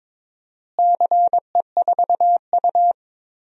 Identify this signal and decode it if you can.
Morse code, heard as a keyed tone, about 22 wpm, 700 Hz. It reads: CE4U
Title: DXCC-100 Call Signs - 4 Characters - Repeated 2x - 22wpm